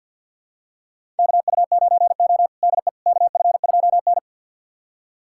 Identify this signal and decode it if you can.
Morse code, heard as a keyed tone, about 50 wpm, 690 Hz. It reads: KU9O DE KF2N
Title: USA Callsign DE Callsign - 4 Characters - 50wpm